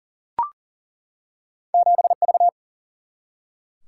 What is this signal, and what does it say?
Morse code, heard as a keyed tone, about 40 wpm, 700 Hz. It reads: 7V
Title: Single-Letter Single-Number 40wpm